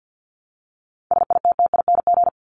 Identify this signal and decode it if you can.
Morse code, heard as a keyed tone, about 50 wpm, 700 Hz. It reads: SITTING
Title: ING Suffix - Top 100 Words with Base Words 50wpm